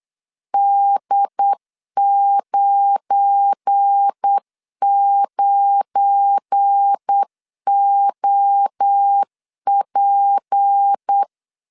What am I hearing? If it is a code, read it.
Morse code, heard as a keyed tone, about 8 wpm, 800 Hz. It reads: D99OP
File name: LEZIONI / 7 GF9Z / 7 GF9 8wpm 3545